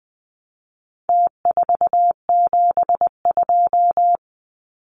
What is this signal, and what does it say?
Morse code, heard as a keyed tone, about 20 wpm, 700 Hz. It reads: T472
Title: Numbers - Contest Style - 4-Digit 20wpm